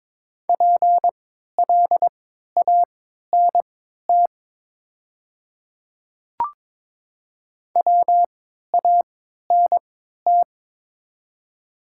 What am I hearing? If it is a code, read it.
Morse code, heard as a keyed tone, about 22 wpm, spaced as for 12 wpm, 690 Hz. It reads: PLANT WANT
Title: 3x Character Spacing - Top 500 Words - 22wpm